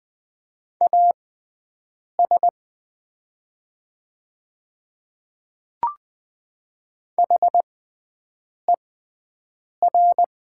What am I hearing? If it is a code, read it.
Morse code, heard as a keyed tone, about 20 wpm, spaced as for 7 wpm, 700 Hz. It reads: AS HER
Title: 6x Character Spacing - Top 100 Words - 20wpm